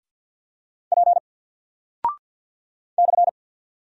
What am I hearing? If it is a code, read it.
Morse code, heard as a keyed tone, about 48 wpm, 700 Hz. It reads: P /
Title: Single Character (Letter:Number:Punctuation:Prosign) 50wpm